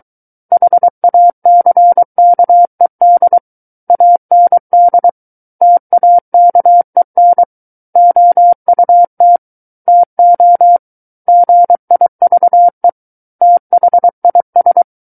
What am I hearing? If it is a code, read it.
Morse code, heard as a keyed tone, about 23 wpm, 700 Hz. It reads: HACKED AND TAKEN OUT TO GIVE THIS
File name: Book One / 14-Chapter 14 - 23 WPM